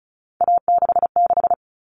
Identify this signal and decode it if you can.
Morse code, heard as a keyed tone, about 34 wpm, 700 Hz. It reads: A66
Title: All Cut Numbers - 3-Digit 35wpm